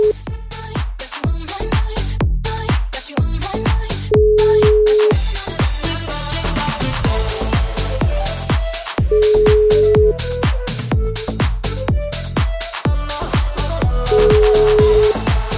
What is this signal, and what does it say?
Ringback tone (Europe, 425 Hz).